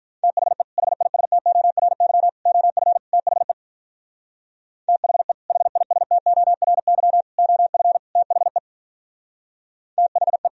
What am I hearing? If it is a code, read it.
Morse code, heard as a keyed tone, about 43 wpm, 700 Hz. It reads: THE HISTORY OF THE THE HISTORY OF THE THE
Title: Speed-Racing - Sets of 4 Words 35wpm